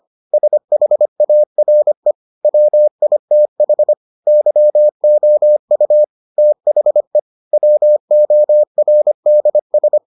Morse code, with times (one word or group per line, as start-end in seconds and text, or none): 0.3-2.1 SHARE
2.4-3.9 WITH
4.3-6.0 YOU
6.4-7.2 THE
7.5-10.0 WORDS